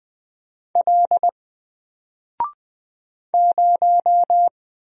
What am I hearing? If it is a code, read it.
Morse code, heard as a keyed tone, about 20 wpm, 700 Hz. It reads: L 0
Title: Single Character (Letter:Number:Punctuation:Prosign) 20wpm